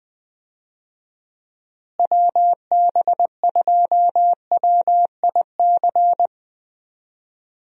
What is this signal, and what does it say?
Morse code, heard as a keyed tone, about 20 wpm, 700 Hz. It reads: WB2WIC
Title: USA Call Signs - 6 Characters - Repeated 3x - 20wpm